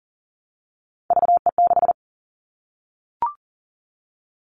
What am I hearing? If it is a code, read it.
Morse code, heard as a keyed tone, about 40 wpm, 700 Hz. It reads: VE6
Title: Callsign Prefixes for Top 100 DXCC - 40wpm